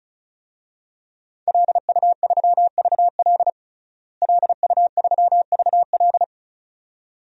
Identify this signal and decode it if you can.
Morse code, heard as a keyed tone, about 36 wpm, 700 Hz. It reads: LU3VL LU3VL
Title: DXCC-100 Call Signs - 5 Characters - Repeated 2x - 35wpm